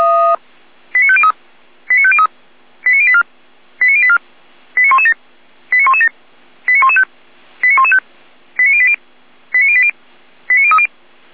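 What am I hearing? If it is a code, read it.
Five-tone selective call (ZVEI), heard as quick runs of five tones, tones of about 70 ms. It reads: 89503 89503 89084 89084 89117 89117 89115 89115 89080 89080 89030
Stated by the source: ZVEI five-tone selective call received by SDR.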